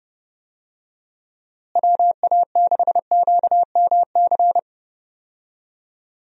Morse code, heard as a keyed tone, about 30 wpm, 700 Hz. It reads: WA6QMC
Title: USA Call Signs - 6 Characters - Repeated 2x - 30wpm